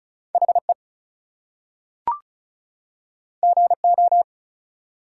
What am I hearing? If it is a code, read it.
Morse code, heard as a keyed tone, about 34 wpm, 700 Hz. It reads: HE GO
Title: Two Letter Words - 35wpm